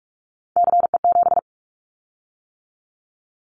Morse code, heard as a keyed tone, about 44 wpm, 700 Hz. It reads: CE7
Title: Callsign Prefixes for Top 100 DXCC - 45wpm